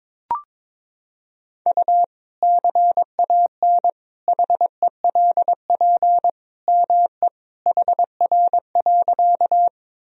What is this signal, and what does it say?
Morse code, heard as a keyed tone, about 22 wpm, 700 Hz. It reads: U CAN HELP ME HR.
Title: Sentences from Top 200 Words and Abbreviations 22wpm